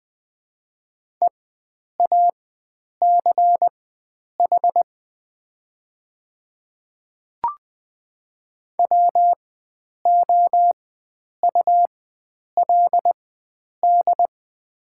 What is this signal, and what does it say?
Morse code, heard as a keyed tone, about 21 wpm, spaced as for 9 wpm, 700 Hz. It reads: EACH WOULD
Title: 4x Character Spacing - Top 100 Words - 20wpm